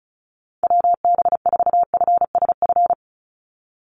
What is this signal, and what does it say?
Morse code, heard as a keyed tone, about 34 wpm, 700 Hz. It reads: WB4FSF
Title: Call Signs - 35wpm